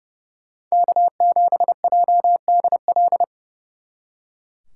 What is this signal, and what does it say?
Morse code, heard as a keyed tone, about 39 wpm, 700 Hz. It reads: K7JDL